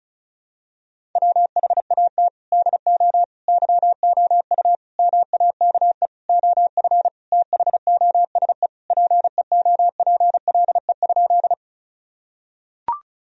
Morse code, heard as a keyed tone, about 34 wpm, 700 Hz. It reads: WHAT DO YOU MAKE OF THOSE PEOPLE?
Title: Sentences from Top 1000 Words - Review Part 1 - 35wpm